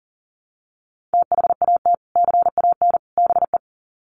Morse code, heard as a keyed tone, about 40 wpm, 700 Hz. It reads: THAT CAN BE
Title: Sets of 3 Words 40wpm